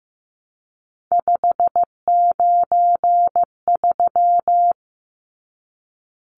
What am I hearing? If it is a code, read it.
Morse code, heard as a keyed tone, about 15 wpm, 700 Hz. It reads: 593
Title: Numbers - 3-Digit 15wpm